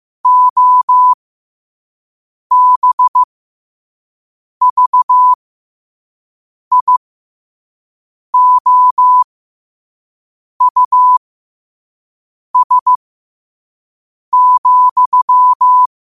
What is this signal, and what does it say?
Morse code, heard as a keyed tone, about 15 wpm, spaced as for 5 wpm, 1000 Hz. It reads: OBVIOUS,
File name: qotdfiles / 5wpm